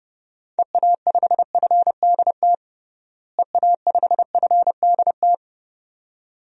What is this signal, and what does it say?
Morse code, heard as a keyed tone, about 30 wpm, 700 Hz. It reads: EA5FDT EA5FDT
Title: DXCC-100 Call Signs - 6 Characters - Repeated 2x - 30wpm